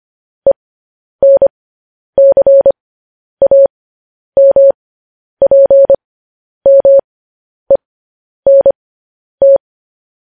Morse code, heard as a keyed tone, about 25 wpm, spaced as for 10 wpm, 550 Hz. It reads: ENCAMPMENT